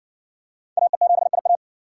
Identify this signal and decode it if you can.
Morse code, heard as a keyed tone, about 57 wpm, 700 Hz. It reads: AE7IA